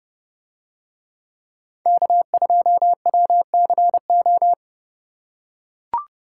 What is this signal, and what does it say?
Morse code, heard as a keyed tone, about 30 wpm, 700 Hz. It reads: K2WCO